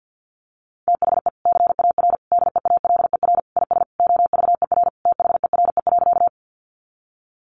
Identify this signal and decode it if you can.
Morse code, heard as a keyed tone, about 50 wpm, 700 Hz. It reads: THE CAR DEALER IS OVER THERE.
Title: Sentences from Top 1100 Words 50wpm